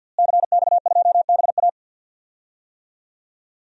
Morse code, heard as a keyed tone, about 51 wpm, 700 Hz. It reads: CX2BA
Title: DXCC-100 Call Signs - 5 Characters - Repeated 2x - 50wpm